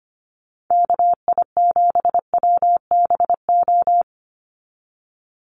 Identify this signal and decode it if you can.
Morse code, heard as a keyed tone, about 25 wpm, 700 Hz. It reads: KI7WBO
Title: USA Call Signs 25wpm